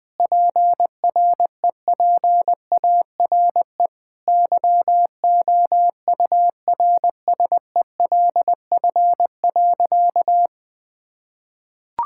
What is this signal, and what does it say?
Morse code, heard as a keyed tone, about 20 wpm, 700 Hz. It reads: PREPARE YOURSELF.
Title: Sentences from Top 1000 Words - Review Part 6 - 20wpm